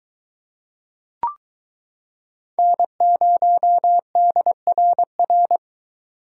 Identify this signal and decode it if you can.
Morse code, heard as a keyed tone, about 22 wpm, 700 Hz. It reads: N0DRR